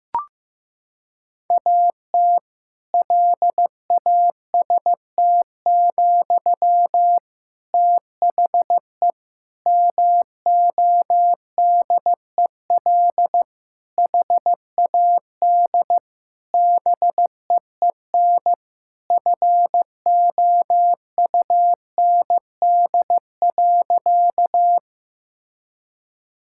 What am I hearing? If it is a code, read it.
Morse code, heard as a keyed tone, about 15 wpm, 700 Hz. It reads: AT LAST, THE MODEL HAD BEEN FOUND.
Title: Sentences from Top 1000 Words - Review Part 5 - 15wpm